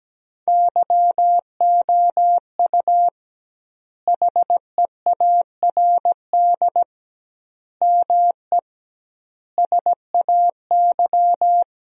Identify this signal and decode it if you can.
Morse code, heard as a keyed tone, about 17 wpm, 700 Hz. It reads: YOU HEARD ME SAY
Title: 2x Word-Spacing - Sentences from Top 500 Words - 17wpm